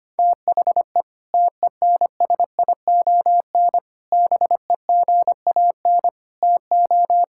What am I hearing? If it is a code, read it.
Morse code, heard as a keyed tone, about 25 wpm, 700 Hz. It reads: THE TENSION BEGAN TO
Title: Sentences from Top 1600 Words 25wpm